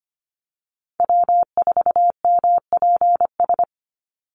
Morse code, heard as a keyed tone, about 25 wpm, 700 Hz. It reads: W4MPS
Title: NA SOTA Chaser Callsigns 25wpm